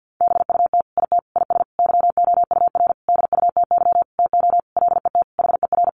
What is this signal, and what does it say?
Morse code, heard as a keyed tone, about 50 wpm, 700 Hz. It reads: BUT IT IS YOUR DUTY TO LET HER
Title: Sentences from Top 1500 Words 50wpm